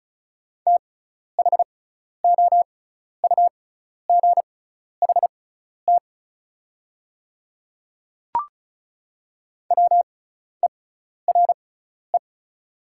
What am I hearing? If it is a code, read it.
Morse code, heard as a keyed tone, about 34 wpm, spaced as for 12 wpm, 700 Hz. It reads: THOUGHT WERE
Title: 6x Character Spacing - Top 500 Words - 35wpm